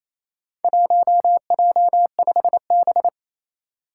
Morse code, heard as a keyed tone, about 28 wpm, 700 Hz. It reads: 1J5B
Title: Four Challenging Letter-Number Combinations 28wpm